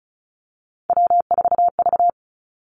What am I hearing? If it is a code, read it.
Morse code, heard as a keyed tone, about 36 wpm, 700 Hz. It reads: W4V
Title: Three Challenging Letter-Number Combinations 35wpm